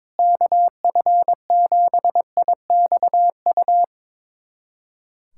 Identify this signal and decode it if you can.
Morse code, heard as a keyed tone, about 22 wpm, 700 Hz. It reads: KF7IXU
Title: USA Callsign DE Callsign - 6 Characters - 22wpm